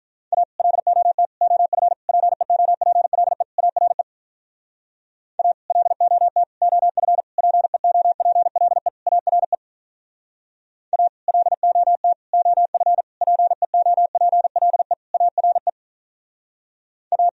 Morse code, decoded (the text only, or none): A LOT OF PEOPLE ARE A LOT OF PEOPLE ARE A LOT OF PEOPLE ARE A